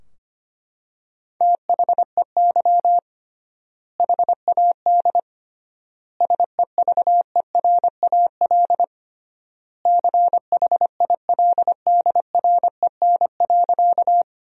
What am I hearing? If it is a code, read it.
Morse code, heard as a keyed tone, about 25 wpm, 700 Hz. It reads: THEY HAD SEVERAL CHILDREN.
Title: 3x Word-Spacing - Sentences from Top 400 Words - 25wpm